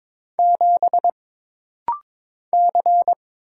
Morse code, heard as a keyed tone, about 22 wpm, 700 Hz. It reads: 7 C
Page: This rapid-fire timing is geared to push your ICR (Instant Character Recognition) skills.